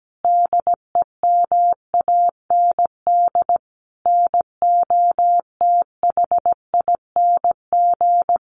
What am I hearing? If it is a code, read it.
Morse code, heard as a keyed tone, about 17 wpm, 700 Hz. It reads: DEMAND NOTHING
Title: Sentences from Top 800 Words 17wpm